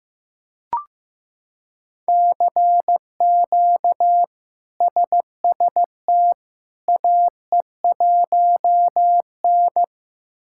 Morse code, heard as a keyed tone, about 15 wpm, 700 Hz. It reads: CQ SST AE1N
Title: K1USN SST QSO Practice - 15wpm